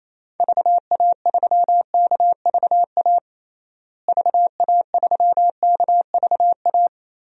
Morse code, heard as a keyed tone, about 28 wpm, 700 Hz. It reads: VA3KVA VA3KVA
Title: DXCC-100 Call Signs - 6 Characters - Repeated 2x - 28wpm